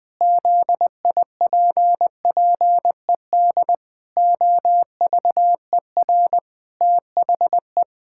Morse code, heard as a keyed tone, about 20 wpm, 700 Hz. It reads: ZIPPED OVER THE
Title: Pangram Challenge v2 - 20wpm